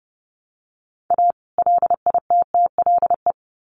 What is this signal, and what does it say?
Morse code, heard as a keyed tone, about 30 wpm, 700 Hz. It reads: A LITTLE
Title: Sets of 2 Words 30wpm